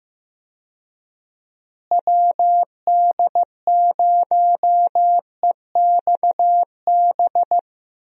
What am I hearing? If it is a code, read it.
Morse code, heard as a keyed tone, about 15 wpm, 700 Hz. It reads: WD0EXB
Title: USA Call Signs - 6 Characters - Repeated 2x - 15wpm